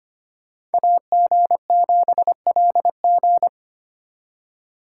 Morse code, heard as a keyed tone, about 30 wpm, 700 Hz. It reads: AG7LG
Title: Speed Racing - USA 2x2 Call Signs 25wpm